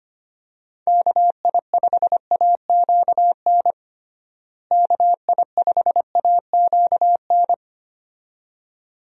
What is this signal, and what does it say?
Morse code, heard as a keyed tone, about 25 wpm, 700 Hz. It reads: KI5AQN KI5AQN
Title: USA Call Signs - 6 Characters - Repeated 2x - 25wpm